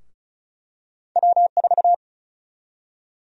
Morse code, heard as a keyed tone, about 36 wpm, 700 Hz. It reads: W4
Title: USA Call Sign Prefixes - Repeated 2x - 35wpm